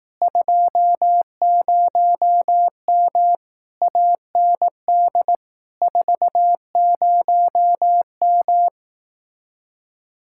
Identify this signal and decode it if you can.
Morse code, heard as a keyed tone, about 18 wpm, 700 Hz. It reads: 20M AND 40M
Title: Speed-Racing Common QSO Phrases Encore - 15wpm